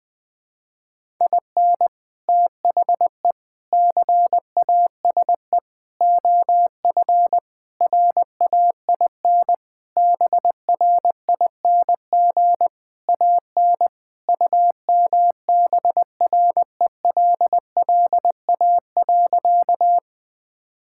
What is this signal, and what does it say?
Morse code, heard as a keyed tone, about 19 wpm, 700 Hz. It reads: IN THE CASE OF RAIN BRING AN UMBRELLA.
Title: Sentences with 3-Word Phrases - 20wpm